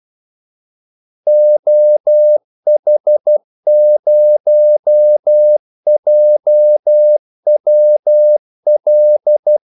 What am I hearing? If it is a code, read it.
Morse code, heard as a keyed tone, about 12 wpm, 600 Hz. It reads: OH0JWL
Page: 2 ekstra ordmellemrum mellem kaldesignalerne